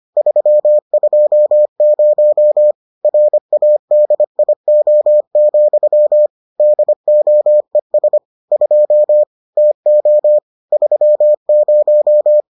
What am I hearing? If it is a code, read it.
Morse code, heard as a keyed tone, about 25 wpm, 600 Hz. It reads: 320 RADIO, DOES 2 TO 30